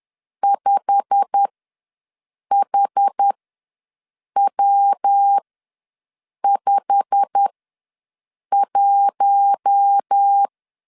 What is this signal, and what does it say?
Morse code, heard as a keyed tone, about 10 wpm, spaced as for 6 wpm, 800 Hz. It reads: 5HW51
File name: LEZIONI / 8 YJ45 / 8 YJ45 ws 10wpm 3843